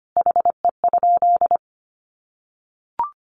Morse code, heard as a keyed tone, about 25 wpm, 700 Hz. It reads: HE?
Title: Sentences from Top 1300 Words 25wpm